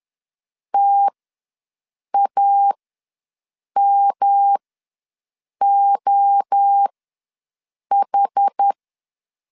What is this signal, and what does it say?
Morse code, heard as a keyed tone, about 10 wpm, spaced as for 6 wpm, 800 Hz. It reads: TAMOH
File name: LEZIONI / 6 PKX0 / 6 PKX ws 10wpm 3706